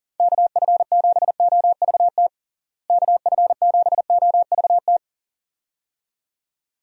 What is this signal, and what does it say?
Morse code, heard as a keyed tone, about 40 wpm, 700 Hz. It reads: KF7OVT KF7OVT
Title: USA Call Signs - 6 Characters - Repeated 2x - 40wpm